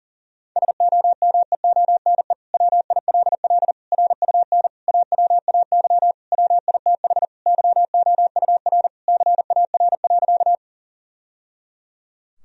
Morse code, heard as a keyed tone, about 40 wpm, 700 Hz. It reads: SOMEONE WILL RUN AWAY WITH YOUR CAR.
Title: Sentences from Top 1000 Words - Review Part 1 - 40wpm